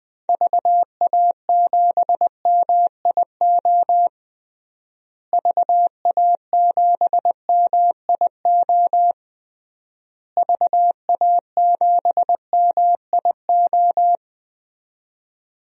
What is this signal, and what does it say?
Morse code, heard as a keyed tone, about 20 wpm, 700 Hz. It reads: VA7MIO VA7MIO VA7MIO
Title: DXCC-100 Call Signs - 6 Characters - Repeated 3x - 20wpm